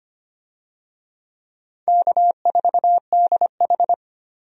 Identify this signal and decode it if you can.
Morse code, heard as a keyed tone, about 25 wpm, 700 Hz. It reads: K4DH
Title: USA Call Signs - 4 Characters - 25wpm